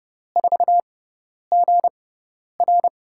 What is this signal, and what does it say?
Morse code, heard as a keyed tone, about 32 wpm, spaced as for 10 wpm, 700 Hz. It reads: 4GR
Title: USA Call Signs - 5 Characters - 6x Character Spacing - 30wpm